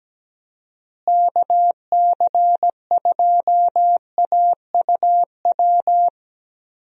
Morse code, heard as a keyed tone, about 17 wpm, 700 Hz. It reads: KC2AUW
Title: USA Call Signs - 6 Characters - Repeated 2x - 17wpm